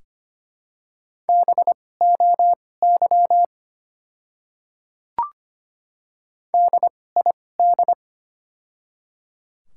Morse code, heard as a keyed tone, about 25 wpm, spaced as for 18 wpm, 700 Hz. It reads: BOY DID
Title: 2x Character Spacing - Top 200 Words - 25wpm